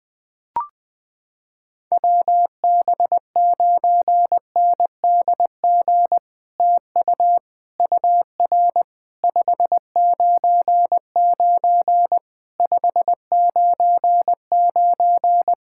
Morse code, heard as a keyed tone, about 20 wpm, 700 Hz. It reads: WB9NDG TU UR 599 599
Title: POTA QSO Practice - 20wpm